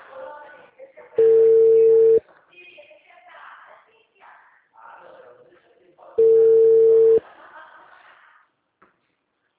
Ringback tone (China, 450 Hz).